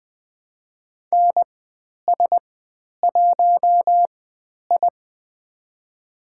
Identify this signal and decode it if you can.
Morse code, heard as a keyed tone, about 20 wpm, spaced as for 10 wpm, 700 Hz. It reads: NS1I
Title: 095 Lesson 16g - Letter c - Call Signs 20wpm